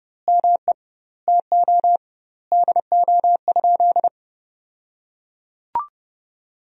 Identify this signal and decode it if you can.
Morse code, heard as a keyed tone, about 30 wpm, 700 Hz. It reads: ME TO DO?
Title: 2x Word-Spacing - Sentences from Top 200 Words - 30wpm